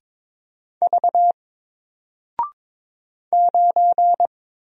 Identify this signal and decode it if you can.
Morse code, heard as a keyed tone, about 22 wpm, 700 Hz. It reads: V 9
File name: Single-Letter-Number-22wpm